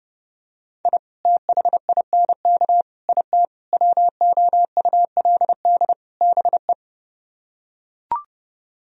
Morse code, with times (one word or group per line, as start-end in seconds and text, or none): 0.8-1.0 I
1.2-2.8 THINK
3.1-3.5 IT
3.7-5.9 WOULD
6.2-6.7 BE